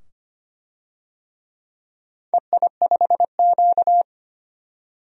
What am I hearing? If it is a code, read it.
Morse code, heard as a keyed tone, about 25 wpm, 700 Hz. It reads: EI5Q
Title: DXCC-100 Call Signs - 4 Characters - 25wpm